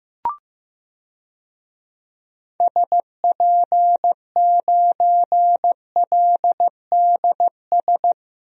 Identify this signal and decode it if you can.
Morse code, heard as a keyed tone, about 15 wpm, 700 Hz. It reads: SP9LDS
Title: DXCC-100 Call Signs - 6 Characters - Repeated 2x - 15wpm